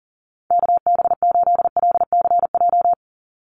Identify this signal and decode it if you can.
Morse code, heard as a keyed tone, about 40 wpm, 700 Hz. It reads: KB8LCJ
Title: USA Call Signs 40wpm